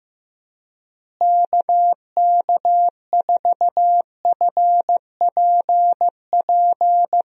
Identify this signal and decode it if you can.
Morse code, heard as a keyed tone, about 15 wpm, 700 Hz. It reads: KK4FPP
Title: USA Call Signs - 6 Characters - Repeated 3x - 15wpm